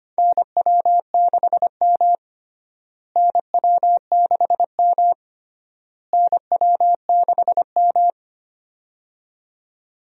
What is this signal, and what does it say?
Morse code, heard as a keyed tone, about 25 wpm, 700 Hz. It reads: NW6M NW6M NW6M
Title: USA Call Signs - 4 Characters - Repeated 3x - 25wpm